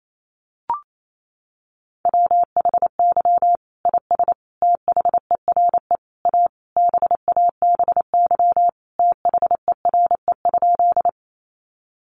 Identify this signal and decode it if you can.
Morse code, heard as a keyed tone, about 28 wpm, 700 Hz. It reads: WHY IS THERE A BABY THERE?
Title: Sentences from Top 1300 Words 28wpm